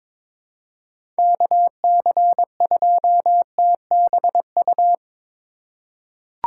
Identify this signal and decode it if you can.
Morse code, heard as a keyed tone, about 22 wpm, 700 Hz. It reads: KC2TBU
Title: USA Call Signs - 6 Characters - 3x Character Spacing - 22wpm